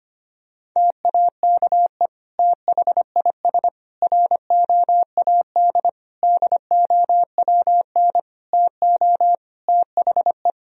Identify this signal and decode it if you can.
Morse code, heard as a keyed tone, about 25 wpm, 700 Hz. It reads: TAKE THIS ROAD DOWN TO THE
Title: Sentences from Top 1000 Words - Review Part 4 - 25wpm